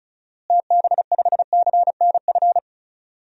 Morse code, heard as a keyed tone, about 36 wpm, 700 Hz. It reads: TB5CNF
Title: DXCC-100 Call Signs - 6 Characters - Repeated 2x - 35wpm